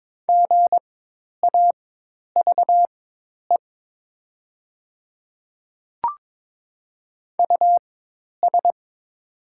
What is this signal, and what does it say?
Morse code, heard as a keyed tone, about 22 wpm, spaced as for 10 wpm, 700 Hz. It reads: GAVE US
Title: 4x Character Spacing - Top 500 Words - 22wpm